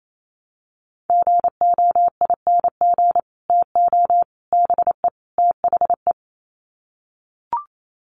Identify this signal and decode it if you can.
Morse code, heard as a keyed tone, about 28 wpm, 700 Hz. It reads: GOING TO BE THE
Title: Sets of 4 Words 28wpm